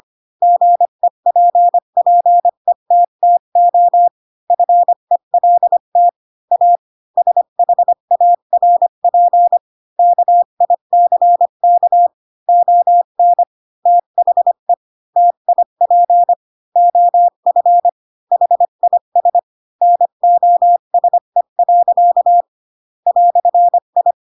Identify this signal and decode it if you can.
Morse code, heard as a keyed tone, about 25 wpm, 700 Hz. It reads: GEPPETTO FELT A SHARP KICK ON THE TIP OF HIS NOSE. "I
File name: Pinocchio / Pinocchio ch3 20 25 ik1hge